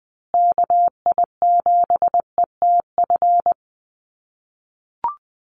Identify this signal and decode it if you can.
Morse code, heard as a keyed tone, about 20 wpm, 700 Hz. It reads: KI7ETF
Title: USA Call Signs Repeated 3x 20wpm